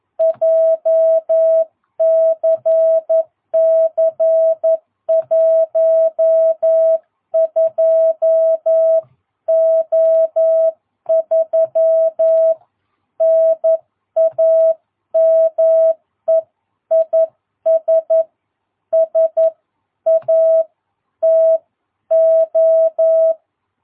Morse code, heard as a keyed tone, about 11 wpm, 660 Hz. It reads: JCC 12O3 NAME IS SATO
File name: jh8cbh / 2022cw